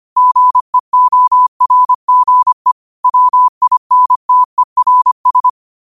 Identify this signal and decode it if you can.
Morse code, heard as a keyed tone, about 25 wpm, 1000 Hz. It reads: GEORGE WINTERS
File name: qotdfiles / 25wpm